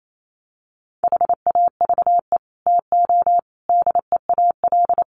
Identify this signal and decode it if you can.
Morse code, heard as a keyed tone, about 28 wpm, 700 Hz. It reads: HAVE TO DEAL
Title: Sets of 5 Words Encore2 28wpm